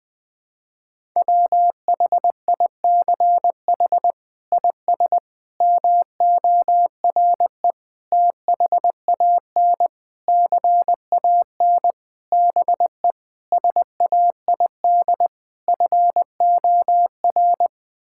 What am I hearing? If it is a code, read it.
Morse code, heard as a keyed tone, about 21 wpm, 700 Hz. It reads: WHICH IS MORE THAN CAN BE SAID FOR
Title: Sets of 8 Words - 20wpm